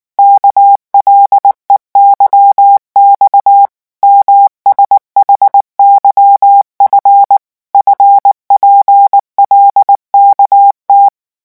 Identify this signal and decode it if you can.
Morse code, heard as a keyed tone, about 19 wpm, 800 Hz. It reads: KLEYX MSHYF FPLKT
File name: morse mp3 / revision7-vitesse 18 mots